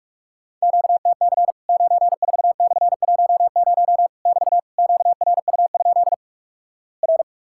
Morse code, heard as a keyed tone, about 45 wpm, 700 Hz. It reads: QTC 94/10 <BT> QRU? R
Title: Worked All Europe DX Contest - QTC Exchange - 45wpm